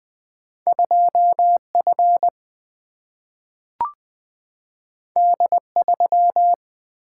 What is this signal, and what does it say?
Morse code, heard as a keyed tone, about 20 wpm, 700 Hz. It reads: 2F D3
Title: Single-Letter Single-Number 20wpm